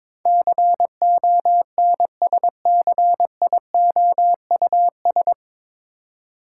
Morse code, heard as a keyed tone, about 22 wpm, 700 Hz. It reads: CONSCIOUS
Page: Morse Code Ninja practice 22wpm